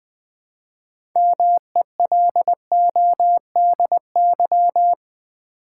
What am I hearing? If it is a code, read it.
Morse code, heard as a keyed tone, about 20 wpm, 700 Hz. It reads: MELODY
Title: 6 Letter Words - First 3 Letters Given - 20wpm